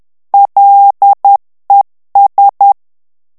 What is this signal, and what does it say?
Morse code, heard as a keyed tone, about 10 wpm, 800 Hz. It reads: LES
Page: Français - texte en clair 1 - vitesse : 10 mots minute